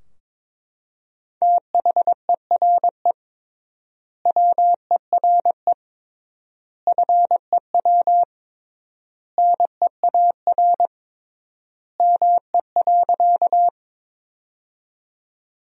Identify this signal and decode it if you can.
Morse code, heard as a keyed tone, about 22 wpm, 700 Hz. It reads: THERE WERE FEW NEAR ME.
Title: 3x Word-Spacing - Sentences from Top 300 Words - 22wpm